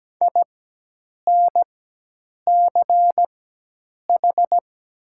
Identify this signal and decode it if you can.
Morse code, heard as a keyed tone, about 17 wpm, spaced as for 8 wpm, 700 Hz. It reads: INCH
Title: 4x Character Spacing - Top 500 Words - 17wpm